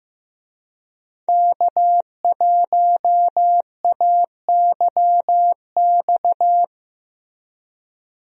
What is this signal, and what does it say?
Morse code, heard as a keyed tone, about 15 wpm, 700 Hz. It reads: K1AYX
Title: USA Call Signs - 5 Characters - Repeated 3x - 15wpm